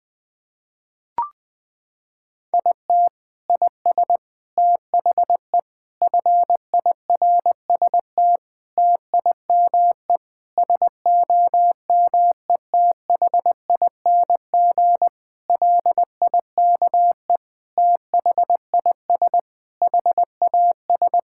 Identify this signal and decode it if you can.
Morse code, heard as a keyed tone, about 20 wpm, 700 Hz. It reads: IT IS THE FIRST TIME SOMETHING LIKE THIS HAS
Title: Sentences with 8-Word Phrases - 20wpm